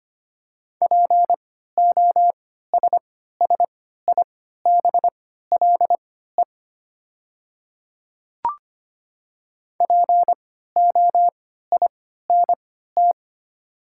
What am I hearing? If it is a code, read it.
Morse code, heard as a keyed tone, about 25 wpm, spaced as for 14 wpm, 700 Hz. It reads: POSSIBLE POINT
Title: 3x Character Spacing - Top 500 Words - 25wpm